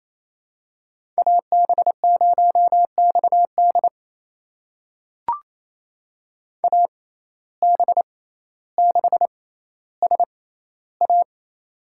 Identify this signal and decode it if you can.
Morse code, heard as a keyed tone, about 28 wpm, spaced as for 14 wpm, 700 Hz. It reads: AB0XD AB6SA
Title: USA Call Signs - 5 Characters - 6x Character Spacing - 28wpm